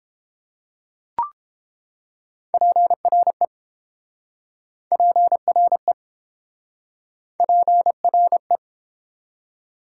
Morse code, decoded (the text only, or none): PRE PRE PRE